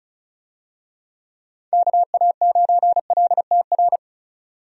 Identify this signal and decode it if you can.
Morse code, heard as a keyed tone, about 36 wpm, 700 Hz. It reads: KA9LTR
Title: USA Call Signs - 6 Characters - 35wpm